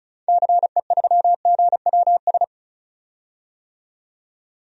Morse code, heard as a keyed tone, about 36 wpm, 700 Hz. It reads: CE3GWS
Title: DXCC-100 Call Signs - 6 Characters - 35wpm